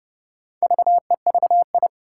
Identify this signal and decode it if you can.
Morse code, heard as a keyed tone, about 30 wpm, 700 Hz. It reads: VEVI
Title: Four Challenging Letter-Number Combinations 30wpm